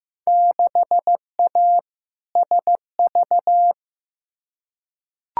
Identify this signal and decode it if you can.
Morse code, heard as a keyed tone, about 15 wpm, 700 Hz. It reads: 6A SV
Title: Field Day Exchanges Encore - 15wpm